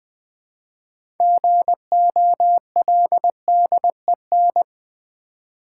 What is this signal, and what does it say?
Morse code, heard as a keyed tone, about 20 wpm, 700 Hz. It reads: GOLDEN
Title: 6 Letter Words - First 3 Letters Given - 20wpm